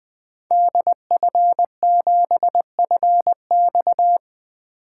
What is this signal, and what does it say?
Morse code, heard as a keyed tone, about 21 wpm, 700 Hz. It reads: DF7FX
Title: 2023 SOTA Activator Call Signs - 20wpm